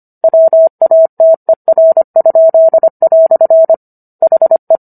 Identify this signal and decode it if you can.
Morse code, heard as a keyed tone, about 25 wpm, 650 Hz. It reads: WATER?" HE